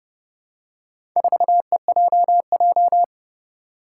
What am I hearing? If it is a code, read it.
Morse code, heard as a keyed tone, about 30 wpm, 700 Hz. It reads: 4EJJ
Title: Four Challenging Letter-Number Combinations 30wpm